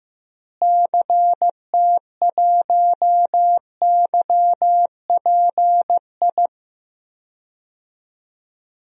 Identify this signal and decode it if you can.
Morse code, heard as a keyed tone, about 15 wpm, 700 Hz. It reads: CT1YPI
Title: DXCC-100 Call Signs - 6 Characters - Repeated 3x - 15wpm